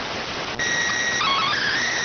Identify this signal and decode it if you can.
SSTV (slow-scan television), PD 290.